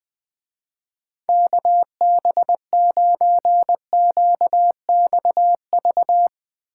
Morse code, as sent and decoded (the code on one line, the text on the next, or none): -.- -... ----. --.- -..- ...-
KB9QXV